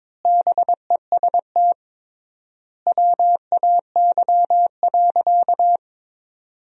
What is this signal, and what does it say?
Morse code, heard as a keyed tone, about 22 wpm, 700 Hz. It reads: BEST WAY.
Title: 3x Word-Spacing - Sentences from Top 400 Words - 22wpm